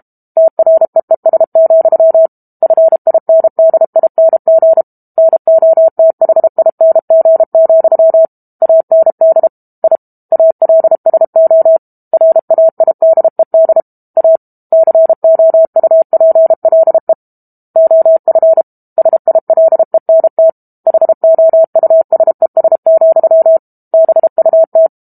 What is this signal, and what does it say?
Morse code, heard as a keyed tone, about 32 wpm, 650 Hz. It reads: TREES, FINDING NOTHING, AND I ALSO RAIDED A COUPLE OF SILENT HOUSES, BUT
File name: Book Two / 06-Chapter 6 - 32 WPM